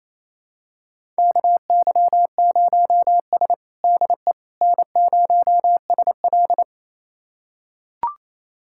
Morse code, heard as a keyed tone, about 27 wpm, 700 Hz. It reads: KY0S DE N0SL
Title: USA Callsign DE Callsign - 4 Characters - 28wpm